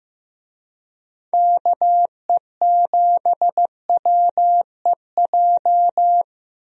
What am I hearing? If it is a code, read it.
Morse code, heard as a keyed tone, about 16 wpm, 700 Hz. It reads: KE7WEJ
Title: USA Call Signs - 6 Characters - 15wpm